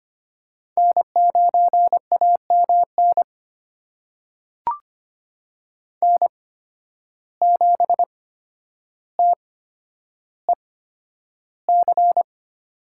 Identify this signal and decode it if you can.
Morse code, heard as a keyed tone, about 25 wpm, spaced as for 11 wpm, 700 Hz. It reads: N9AMN N7TEC
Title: USA Call Signs - 5 Characters - 8x Character Spacing - 25wpm